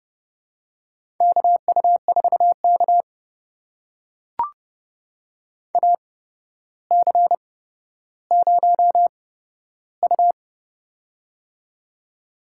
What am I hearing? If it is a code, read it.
Morse code, heard as a keyed tone, about 30 wpm, spaced as for 13 wpm, 700 Hz. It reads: KU4K AC0U
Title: USA Call Signs - 4 Characters - 8x Character Spacing - 30wpm